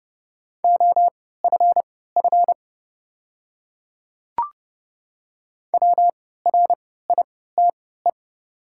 Morse code, heard as a keyed tone, about 30 wpm, spaced as for 17 wpm, 700 Hz. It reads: OFF WRITE
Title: 3x Character Spacing - Top 200 Words - 30wpm